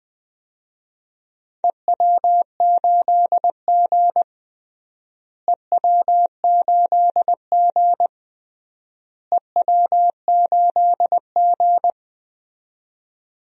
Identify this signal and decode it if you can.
Morse code, heard as a keyed tone, about 20 wpm, 700 Hz. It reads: EW8G EW8G EW8G
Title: DXCC-100 Call Signs - 4 Characters - Repeated 3x - 20wpm